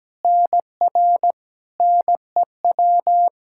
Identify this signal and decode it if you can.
Morse code, heard as a keyed tone, about 17 wpm, 700 Hz. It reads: NR NEW
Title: K1USN SST QSO Practice 17wpm